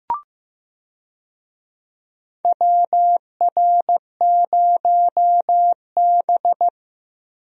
Morse code, heard as a keyed tone, about 15 wpm, 700 Hz. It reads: WR0B
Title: USA Call Signs - 4 Characters - Repeated 2x - 15wpm